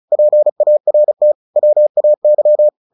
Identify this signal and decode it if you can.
Morse code, heard as a keyed tone, about 35 wpm, 600 Hz. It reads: PART WAY
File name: the old man and the sea 2 35wpm